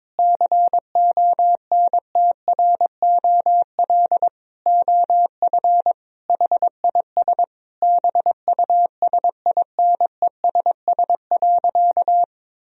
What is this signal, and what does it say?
Morse code, heard as a keyed tone, about 22 wpm, 700 Hz. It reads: CONTROL OF HIS BUSINESS.
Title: Sentences from Top 1000 Words - Review Part 3 - 22wpm